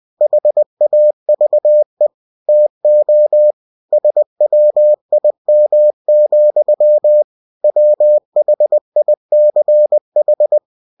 Morse code, heard as a keyed tone, about 20 wpm, 600 Hz. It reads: HAVE TO SWIM, WHICH
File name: cw-books / winnie-the-pooh / winnie-20wpm-page-0093